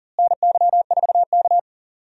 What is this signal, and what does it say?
Morse code, heard as a keyed tone, about 40 wpm, 700 Hz. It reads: NY4K
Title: USA Call Signs - 4 Characters - Repeated 2x - 40wpm